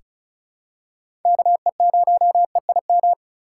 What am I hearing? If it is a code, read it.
Morse code, heard as a keyed tone, about 36 wpm, 700 Hz. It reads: KE0EIM
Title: USA Call Signs - 6 Characters - 3x Character Spacing - 35wpm